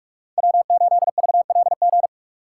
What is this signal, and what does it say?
Morse code, heard as a keyed tone, about 43 wpm, 700 Hz. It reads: W8VLG
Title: USA Call Signs - 5 Characters - 6x Character Spacing - 45wpm